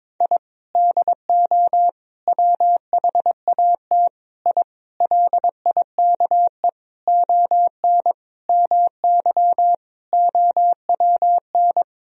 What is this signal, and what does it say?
Morse code, heard as a keyed tone, about 22 wpm, 700 Hz. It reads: I DO WHAT I LIKE ON MY OWN
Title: Sentences from Top 1000 Words - Review Part 2 - 22wpm